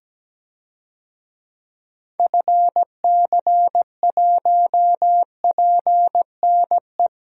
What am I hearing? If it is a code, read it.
Morse code, heard as a keyed tone, about 17 wpm, 700 Hz. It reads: FC1PNE
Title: DXCC-100 Call Signs - 6 Characters - Repeated 2x - 17wpm